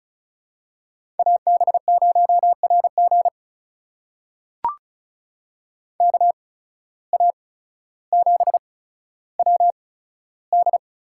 Morse code, heard as a keyed tone, about 36 wpm, spaced as for 15 wpm, 700 Hz. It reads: AB0RG KA7WD
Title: USA Call Signs - 8x Character Spacing - 35wpm